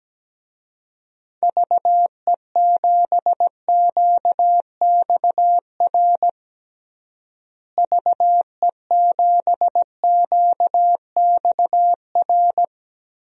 Morse code, heard as a keyed tone, about 17 wpm, 700 Hz. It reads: VE7QXR VE7QXR
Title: DXCC-100 Call Signs - 6 Characters - Repeated 2x - 17wpm